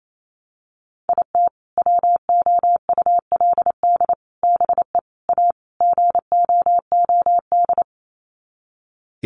Morse code, heard as a keyed tone, about 28 wpm, 700 Hz. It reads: IT WOULD BE A GOOD
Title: Sets of 5 Words Encore2 28wpm